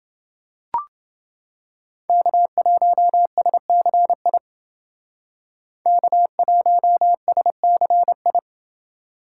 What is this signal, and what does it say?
Morse code, heard as a keyed tone, about 28 wpm, 700 Hz. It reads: K1SCI K1SCI